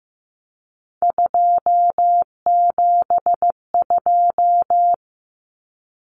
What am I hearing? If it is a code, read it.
Morse code, heard as a keyed tone, about 15 wpm, 700 Hz. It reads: 272
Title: Numbers - 3-Digit 15wpm